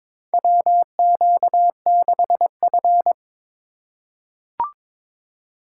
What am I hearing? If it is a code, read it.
Morse code, heard as a keyed tone, about 22 wpm, 700 Hz. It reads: WQ6F
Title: USA Call Signs - 4 Characters - 8x Character Spacing - 22wpm